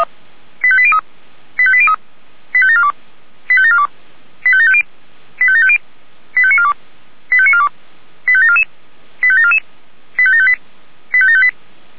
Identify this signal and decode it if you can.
Five-tone selective call (ZVEI), heard as quick runs of five tones, tones of about 70 ms. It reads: 85803 85803 85742 85742 85750 85750 85942 85942 85744 85744 85758 85758